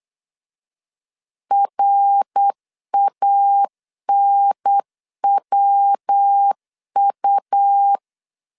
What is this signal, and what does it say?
Morse code, heard as a keyed tone, about 8 wpm, 800 Hz. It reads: RANWU
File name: CW / DET / 1 3 1500C S C 8wpm 4214